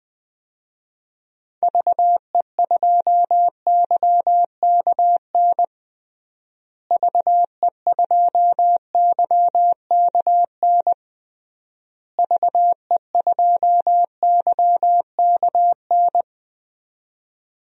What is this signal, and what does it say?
Morse code, heard as a keyed tone, about 20 wpm, 700 Hz. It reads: VE2YKN VE2YKN VE2YKN
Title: DXCC-100 Call Signs - 6 Characters - Repeated 3x - 20wpm